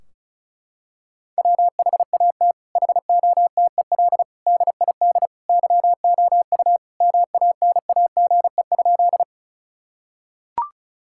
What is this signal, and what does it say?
Morse code, heard as a keyed tone, about 36 wpm, 700 Hz. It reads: WHAT HOTEL DID YOU MANAGE?
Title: Sentences from Top 1600 Words 35wpm